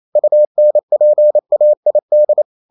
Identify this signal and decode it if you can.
Morse code, heard as a keyed tone, about 28 wpm, 600 Hz. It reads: UNPAID
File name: Never 0027wpm